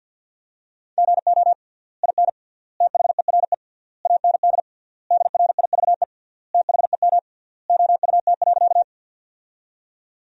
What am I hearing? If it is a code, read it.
Morse code, heard as a keyed tone, about 47 wpm, 700 Hz. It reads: GO IN THERE AND DRIVE THEM OUT.
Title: 3x Word-Spacing - Sentences from Top 500 Words - 50wpm